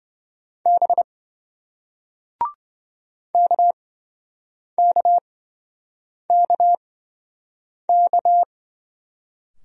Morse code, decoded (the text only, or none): B KKKK